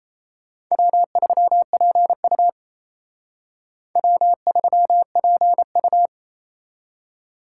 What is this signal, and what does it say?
Morse code, heard as a keyed tone, about 35 wpm, 700 Hz. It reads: W3PU W3PU
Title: Speed Racing - USA 1x2 Call Signs 28wpm